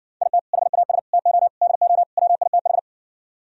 Morse code, heard as a keyed tone, about 70 wpm, 700 Hz. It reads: IT HAS TO DO WITH